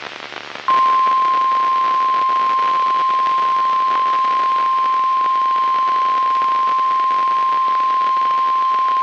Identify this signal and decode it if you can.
Emergency Alert System (SAME) tones (the 1050 Hz attention tone), with noise. It is heard from 1.1 to 8.1 s.